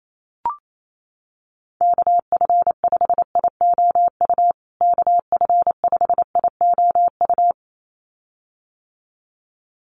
Morse code, heard as a keyed tone, about 28 wpm, 700 Hz. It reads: KF5IOU KF5IOU
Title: USA Call Signs Repeated 2x 28wpm